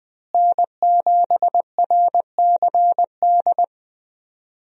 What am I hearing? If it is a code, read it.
Morse code, heard as a keyed tone, about 21 wpm, 700 Hz. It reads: N7RCD
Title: USA Call Signs - 5 Characters - Repeated 3x - 20wpm